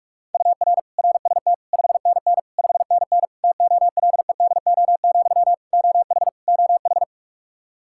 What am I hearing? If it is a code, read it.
Morse code, heard as a keyed tone, about 46 wpm, 700 Hz. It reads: UR RST 5NN 5NN TOLEDO, OH OH
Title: Practice QSO - Exchange 1 - Beginning - 45wpm